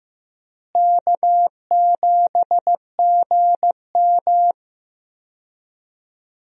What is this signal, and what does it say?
Morse code, heard as a keyed tone, about 15 wpm, 700 Hz. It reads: K7GM
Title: USA Call Signs - 4 Characters - Repeated 2x - 15wpm